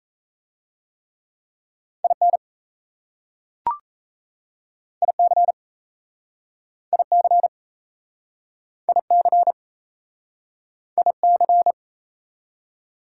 Morse code, heard as a keyed tone, about 38 wpm, 690 Hz. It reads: IN IC IC IC IC